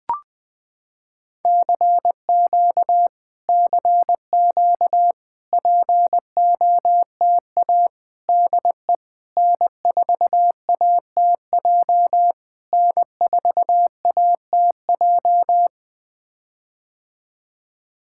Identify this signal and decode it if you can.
Morse code, heard as a keyed tone, about 20 wpm, 700 Hz. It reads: CQ CQ POTA DE N4ATJ N4ATJ
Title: POTA QSO Practice - 20wpm